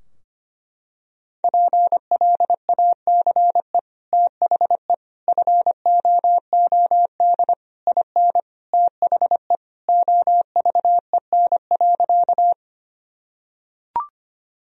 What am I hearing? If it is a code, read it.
Morse code, heard as a keyed tone, about 25 wpm, 700 Hz. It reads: PLACE THE FOOD IN THE OVEN.
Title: Sentences from Top 1000 Words - Review Part 4 - 25wpm